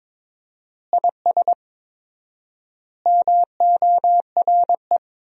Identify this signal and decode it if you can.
Morse code, heard as a keyed tone, about 22 wpm, 700 Hz. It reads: IS MORE
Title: 4x Word-Spacing - Sentences from Top 500 Words - 22wpm